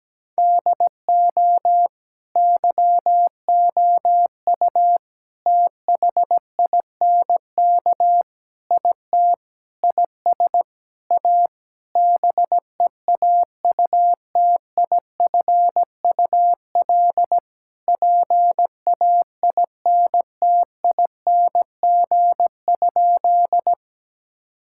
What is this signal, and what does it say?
Morse code, heard as a keyed tone, about 17 wpm, 700 Hz. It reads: DO YOU THINK IT IS A BEAUTIFUL PAINTING?
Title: Sentences from Top 1000 Words - Review Part 6 - 17wpm